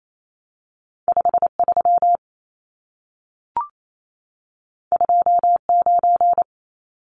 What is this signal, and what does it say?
Morse code, heard as a keyed tone, about 28 wpm, 700 Hz. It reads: 53 29
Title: Numbers - 2-Digit 28wpm